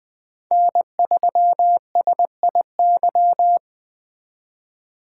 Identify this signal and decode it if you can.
Morse code, heard as a keyed tone, about 21 wpm, 700 Hz. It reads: N3SIY